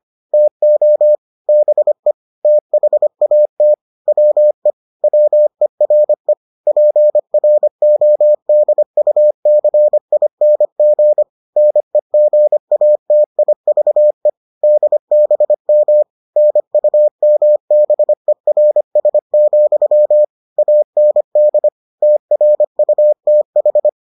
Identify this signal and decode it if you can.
Morse code, heard as a keyed tone, about 25 wpm, 600 Hz. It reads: TO BE THAT WE WERE PRODUCING NEGATIVE DBM NUMBERS, AND TRUTH